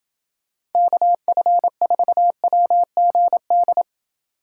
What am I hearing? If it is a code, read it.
Morse code, heard as a keyed tone, about 31 wpm, 700 Hz. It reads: KF4WGD